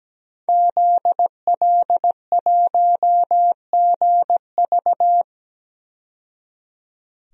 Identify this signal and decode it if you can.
Morse code, heard as a keyed tone, about 17 wpm, 700 Hz. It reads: ZL1GV
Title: DXCC-100 Call Signs - 5 Characters - 17wpm